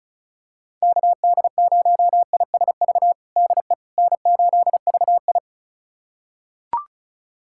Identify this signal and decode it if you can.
Morse code, heard as a keyed tone, about 34 wpm, 700 Hz. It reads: KD0ISV DE N8VI
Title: USA Callsign DE Callsign 35wpm